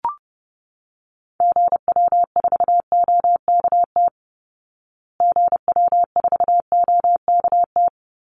Morse code, heard as a keyed tone, about 30 wpm, 700 Hz. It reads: GW4OKT GW4OKT
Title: Call Signs Repeated 2x - 30wpm